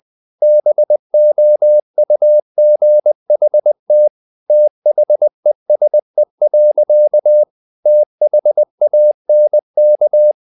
Morse code, heard as a keyed tone, about 20 wpm, 600 Hz. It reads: BOUGHT THESE. THANK
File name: the old man and the sea / the old man and the sea part 1 20wpm